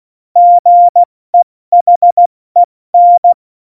Morse code, heard as a keyed tone, about 16 wpm, spaced as for 14 wpm, 700 Hz. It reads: GEHEN
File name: Morsekurs / Witze / Witz 26. Juni 2018, 70 F80